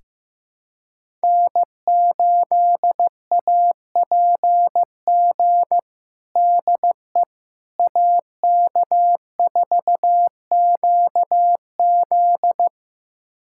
Morse code, heard as a keyed tone, about 15 wpm, 700 Hz. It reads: N8APG DE AK4QZ
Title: USA Callsign DE Callsign - 5 Characters - 15wpm